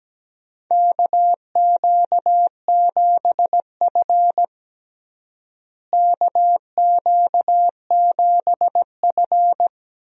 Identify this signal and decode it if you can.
Morse code, heard as a keyed tone, about 17 wpm, 700 Hz. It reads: KQ7F KQ7F
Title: USA Call Signs - 4 Characters - Repeated 2x - 17wpm